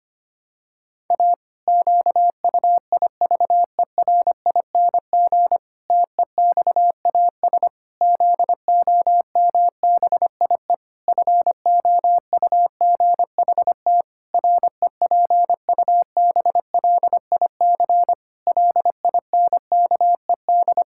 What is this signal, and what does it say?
Morse code, heard as a keyed tone, about 25 wpm, 700 Hz. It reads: A QUIVERING TEXAS ZOMBIE FOUGHT REPUBLIC LINKED
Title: Pangram Challenge v2 - 25wpm